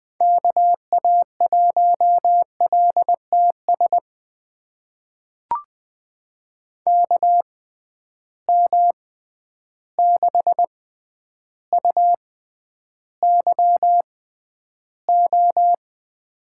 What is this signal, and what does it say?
Morse code, heard as a keyed tone, about 20 wpm, spaced as for 10 wpm, 700 Hz. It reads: KA1LTS KM6UYO
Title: USA Call Signs - 6 Characters - 6x Character Spacing - 20wpm